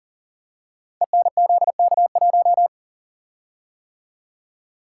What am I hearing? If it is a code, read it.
Morse code, heard as a keyed tone, about 41 wpm, 700 Hz. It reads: ENZK1
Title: Code Groups - 5 Letter:Numbers - 40wpm